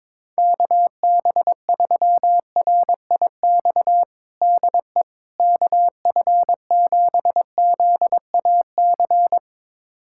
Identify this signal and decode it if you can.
Morse code, heard as a keyed tone, about 22 wpm, 700 Hz. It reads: KB3RIX DE KF7ZAC
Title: USA Callsign DE Callsign - 6 Characters - 22wpm